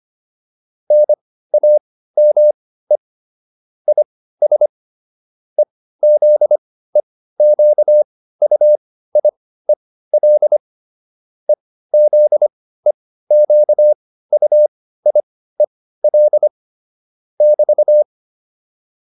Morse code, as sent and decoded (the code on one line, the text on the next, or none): -. .- -- . / .. ... / . --.. --.- ..- .. .-.. / . --.. --.- ..- .. .-.. / -...-
NAME IS EZEQUIEL EZEQUIEL <BT>